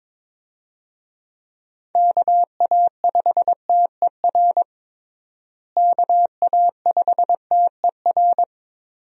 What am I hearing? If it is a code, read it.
Morse code, heard as a keyed tone, about 22 wpm, 700 Hz. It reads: KA5TER KA5TER
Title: USA Call Signs - 6 Characters - Repeated 3x - 22wpm